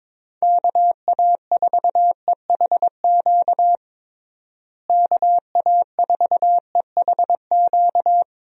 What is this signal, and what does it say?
Morse code, heard as a keyed tone, about 22 wpm, 700 Hz. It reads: KA4EHQ KA4EHQ
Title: USA Call Signs - 6 Characters - Repeated 3x - 22wpm